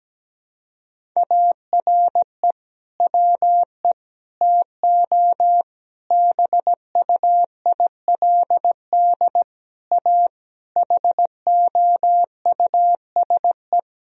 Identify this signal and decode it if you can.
Morse code, heard as a keyed tone, about 17 wpm, 700 Hz. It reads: ARE WE TO BUILD A HOUSE
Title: Sentences from Top 1000 Words - Review Part 1 - 17wpm